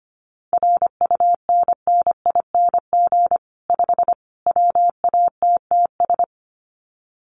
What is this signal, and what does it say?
Morse code, heard as a keyed tone, about 25 wpm, 700 Hz. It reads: RUNNING 5 WATTS
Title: Common QSO Phrases 25wpm